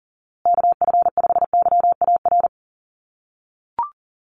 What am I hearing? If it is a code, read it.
Morse code, heard as a keyed tone, about 40 wpm, 700 Hz. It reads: KF5YAR
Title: USA Call Signs Repeated 3x 40wpm